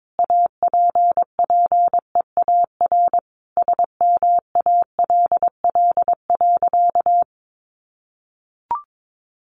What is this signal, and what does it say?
Morse code, heard as a keyed tone, about 22 wpm, 700 Hz. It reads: APPEAR SMALL.
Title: Sentences from Top 1500 Words 22wpm